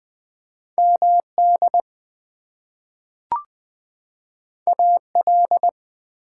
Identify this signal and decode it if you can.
Morse code, heard as a keyed tone, about 20 wpm, 700 Hz. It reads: MD AL
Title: US States and Canadian Provinces 20wpm